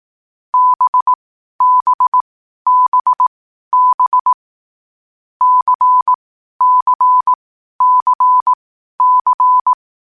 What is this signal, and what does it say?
Morse code, heard as a keyed tone, about 18 wpm, spaced as for 12 wpm, 1000 Hz. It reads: BBBB CCCC
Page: Hier findet Ihr das Morsealphabet in langsamer Geschwindigkeit zum Hören. Dabei ist jeder Buchstabe vier Mal hintereinander aufgenommen worden, danach kommt direkt der nächste.